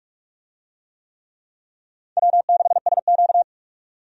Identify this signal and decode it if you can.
Morse code, heard as a keyed tone, about 46 wpm, 700 Hz. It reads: W6SQ
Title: USA Call Signs - 4 Characters - 45wpm